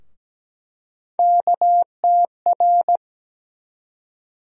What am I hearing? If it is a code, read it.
Morse code, heard as a keyed tone, about 17 wpm, 700 Hz. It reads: KTR
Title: USA Call Sign Postfixes - 3 Letters 17wpm